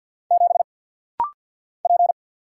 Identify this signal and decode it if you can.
Morse code, heard as a keyed tone, about 48 wpm, 700 Hz. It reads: 7 P
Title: Single Letter-Number Alt - Rapid-Fire 50wpm